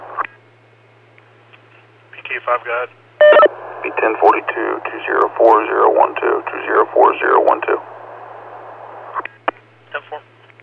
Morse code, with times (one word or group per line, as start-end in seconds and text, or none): none